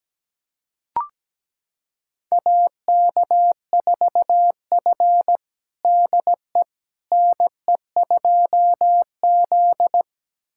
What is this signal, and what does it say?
Morse code, heard as a keyed tone, about 17 wpm, 700 Hz. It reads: AK4F DE NE2Z
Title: USA Callsign DE Callsign - 4 Characters - 17wpm